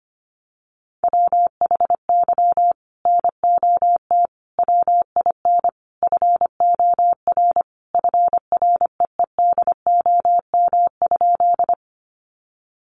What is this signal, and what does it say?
Morse code, heard as a keyed tone, about 25 wpm, 700 Hz. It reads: WHY NOT WIN FOR FREEDOM?
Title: Sentences from Top 1500 Words 25wpm